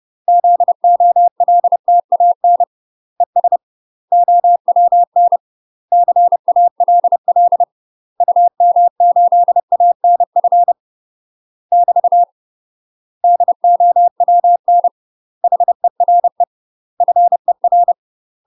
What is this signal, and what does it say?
Morse code, heard as a keyed tone, about 30 wpm, 700 Hz. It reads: ZOLTAN ES OWN CALL UM8ANF <BT> DOWN HERE FER
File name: UK MASTER QSOs TRAINING / 30WPM / UK MASTER QSOs 0027